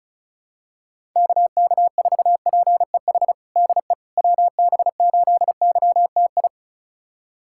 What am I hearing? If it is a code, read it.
Morse code, heard as a keyed tone, about 34 wpm, 700 Hz. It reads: KK4PEH DE WB8YTI
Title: USA Callsign DE Callsign - 6 Characters - 35wpm